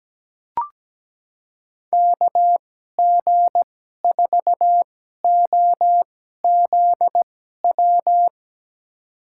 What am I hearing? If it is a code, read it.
Morse code, heard as a keyed tone, about 17 wpm, spaced as for 12 wpm, 700 Hz. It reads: KG4OZW
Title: USA Call Signs - 6 Characters - 2x Character Spacing - 17wpm